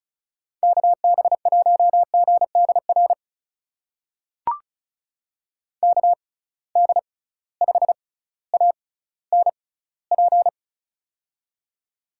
Morse code, heard as a keyed tone, about 36 wpm, spaced as for 18 wpm, 700 Hz. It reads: KB1GDR KD5ANP
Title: USA Call Signs - 6 Characters - 6x Character Spacing - 35wpm